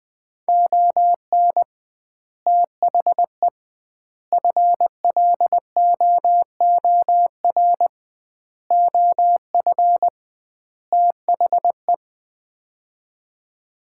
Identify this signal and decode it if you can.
Morse code, heard as a keyed tone, about 21 wpm, 700 Hz. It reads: ON THE FLOOR OF THE
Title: 2x Word-Spacing - Sets of 5 Words - 20wpm